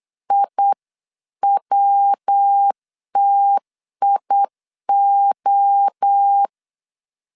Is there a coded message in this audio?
Yes: Morse code from 0.3 to 6.5 s, heard as a keyed tone, about 8 wpm, 800 Hz. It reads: IWTIO